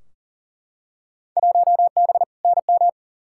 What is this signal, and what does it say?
Morse code, heard as a keyed tone, about 40 wpm, 700 Hz. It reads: 1B NM
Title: Field Day Exchanges Encore - 40wpm